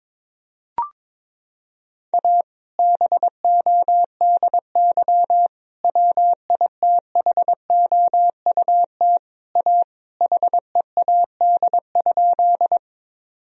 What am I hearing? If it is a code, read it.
Morse code, heard as a keyed tone, about 21 wpm, 700 Hz. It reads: A BODY WITHOUT A HEAD?
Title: Sentences from Top 1000 Words - Review Part 3 - 22wpm